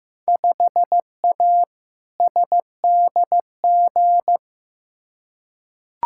Morse code, heard as a keyed tone, about 15 wpm, 700 Hz. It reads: 5A SDG
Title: Field Day Exchanges Encore - 15wpm